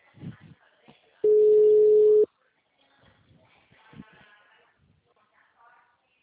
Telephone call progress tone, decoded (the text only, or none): ringback tone